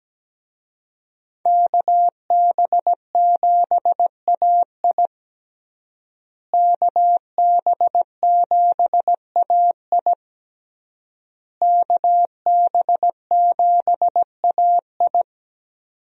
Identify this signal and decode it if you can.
Morse code, heard as a keyed tone, about 17 wpm, 700 Hz. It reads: KB7AI KB7AI KB7AI
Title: USA Call Signs - 5 Characters - Repeated 3x - 17wpm